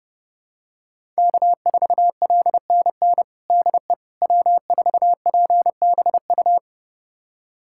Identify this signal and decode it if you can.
Morse code, heard as a keyed tone, about 30 wpm, 700 Hz. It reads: K4LNN DE W4PBU
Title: USA Callsign DE Callsign - 5 Characters - 30wpm